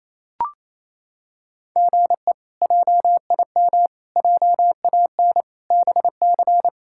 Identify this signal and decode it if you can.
Morse code, heard as a keyed tone, about 28 wpm, 700 Hz. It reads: GE JIM JAN BC
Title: K1USN SST QSO Practice - 28wpm